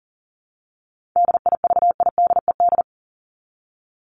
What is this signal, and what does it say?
Morse code, heard as a keyed tone, about 40 wpm, 700 Hz. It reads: DIVIDED
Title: ED Suffix - Top 100 Words with Base Words 40wpm